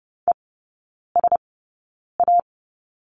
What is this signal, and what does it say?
Morse code, heard as a keyed tone, about 30 wpm, spaced as for 9 wpm, 700 Hz. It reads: ESA
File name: Lesson 16e - Letter c - Three Characters 30wpm